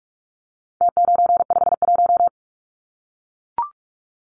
Morse code, heard as a keyed tone, about 44 wpm, 700 Hz. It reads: T951
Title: Numbers - Contest Style - 4-Digit 45wpm